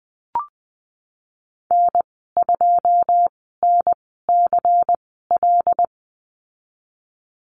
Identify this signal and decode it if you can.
Morse code, heard as a keyed tone, about 20 wpm, spaced as for 15 wpm, 700 Hz. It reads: N2NCL
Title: 140 Lesson 23g - Letter f - Call Signs 20wpm